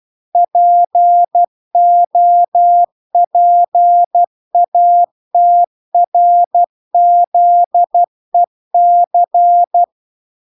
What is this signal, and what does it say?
Morse code, heard as a keyed tone, about 12 wpm, 700 Hz. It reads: POPATRZEC
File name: recordings / 12wpm0000